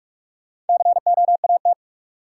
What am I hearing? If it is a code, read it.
Morse code, heard as a keyed tone, about 46 wpm, 700 Hz. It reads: COAT
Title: Sentences from Top 1600 Words - New Word Drill Alt - 45wpm